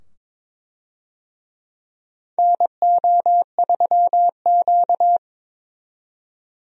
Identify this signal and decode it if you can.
Morse code, heard as a keyed tone, about 22 wpm, 700 Hz. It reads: NO3Q
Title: USA Call Signs - 4 Characters - 22wpm